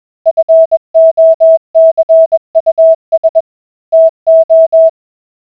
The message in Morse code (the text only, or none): FOCUS TO